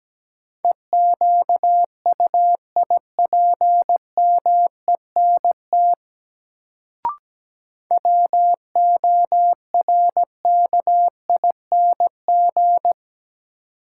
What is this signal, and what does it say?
Morse code, heard as a keyed tone, about 17 wpm, 700 Hz. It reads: EQUIPMENT WORKING
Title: Sentences from Top 700 Words - New Word Drill - 17wpm